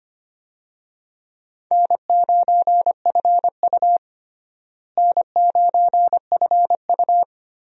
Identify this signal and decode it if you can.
Morse code, heard as a keyed tone, about 25 wpm, 700 Hz. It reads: N9FU N9FU
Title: USA Call Signs - 4 Characters - Repeated 3x - 25wpm